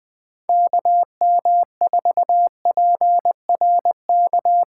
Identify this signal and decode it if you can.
Morse code, heard as a keyed tone, about 23 wpm, 700 Hz. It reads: KM4PRK
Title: Speed Racing - USA Call Signs 20wpm